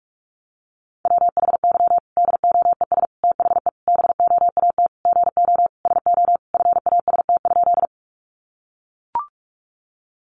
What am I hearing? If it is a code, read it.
Morse code, heard as a keyed tone, about 44 wpm, 700 Hz. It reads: WHY DOES THE BOAT GO SO FAST?
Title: Sentences from Top 500 Words 45wpm